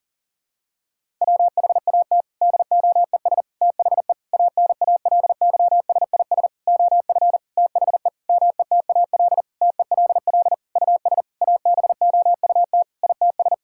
Morse code, heard as a keyed tone, about 40 wpm, 700 Hz. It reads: WHAT DOES THE ANALYSIS OF THE METAL TELL US ABOUT ITS
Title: Sentences from Top 1000 Words - Review Part 6 - 40wpm